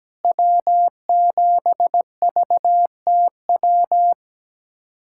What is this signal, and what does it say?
Morse code, heard as a keyed tone, about 17 wpm, 700 Hz. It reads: W7VTW
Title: USA Callsign DE Callsign 17wpm